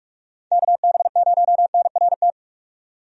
Morse code, heard as a keyed tone, about 46 wpm, 700 Hz. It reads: KB0NRT
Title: USA Call Signs - 6 Characters - 45wpm